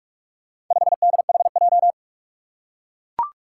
Morse code, heard as a keyed tone, about 44 wpm, 700 Hz. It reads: 5DHJ
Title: Four Challenging Letter-Number Combinations 45wpm